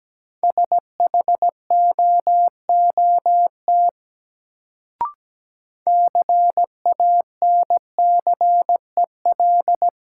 Morse code, heard as a keyed tone, about 17 wpm, 700 Hz. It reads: SHOOT CANCEL
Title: Sentences from Top 1200 Words - New Word Drill Alt - 17wpm